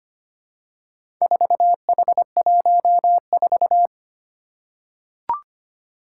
Morse code, heard as a keyed tone, about 25 wpm, 700 Hz. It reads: 4H14
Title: Four Challenging Letter-Number Combinations 25wpm